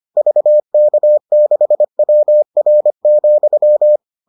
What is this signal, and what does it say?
Morse code, heard as a keyed tone, about 25 wpm, 600 Hz. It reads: VK6WR,